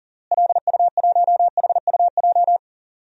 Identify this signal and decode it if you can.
Morse code, heard as a keyed tone, about 37 wpm, 700 Hz. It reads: LU1HUJ
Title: DXCC-100 Call Signs - 6 Characters - Repeated 3x - 40wpm